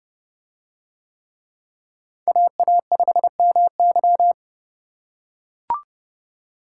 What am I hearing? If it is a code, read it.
Morse code, heard as a keyed tone, about 31 wpm, 700 Hz. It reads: AA5MY